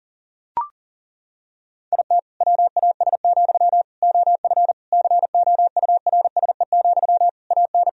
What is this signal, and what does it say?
Morse code, heard as a keyed tone, about 40 wpm, 700 Hz. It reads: IT WAS, OF COURSE, AN
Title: Sentences from Top 1000 Words - Review Part 3 - 40wpm